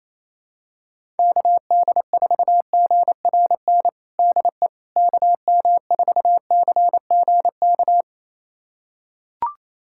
Morse code, heard as a keyed tone, about 28 wpm, 700 Hz. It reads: KD4GRN DE KM4CGK
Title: USA Callsign DE Callsign - 6 Characters - 28wpm